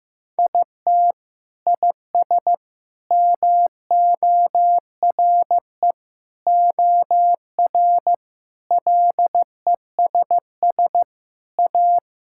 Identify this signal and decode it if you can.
Morse code, heard as a keyed tone, about 15 wpm, 700 Hz. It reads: IT IS MORE OR LESS A
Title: Sentences from Top 1000 Words - Review Part 5 - 15wpm